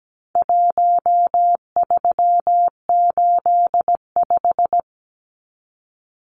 Morse code, heard as a keyed tone, about 17 wpm, 700 Hz. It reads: 1385
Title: Numbers - 4-Digit 17wpm